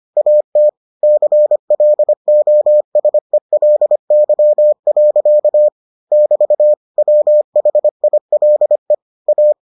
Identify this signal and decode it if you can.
Morse code, heard as a keyed tone, about 25 wpm, 600 Hz. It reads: AT CLOSELY. <BT> WHILE A